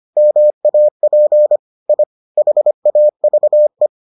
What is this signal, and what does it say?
Morse code, heard as a keyed tone, about 25 wpm, 600 Hz. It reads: MAP I HAVE